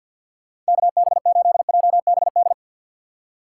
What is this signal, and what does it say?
Morse code, heard as a keyed tone, about 59 wpm, 700 Hz. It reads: KB8JBD
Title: Speed Racing - USA 2x3 Call Signs 50wpm